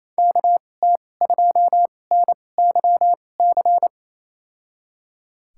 Morse code, heard as a keyed tone, about 28 wpm, spaced as for 20 wpm, 700 Hz. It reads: KT2NYC
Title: USA Call Signs - 6 Characters - 2x Character Spacing - 28wpm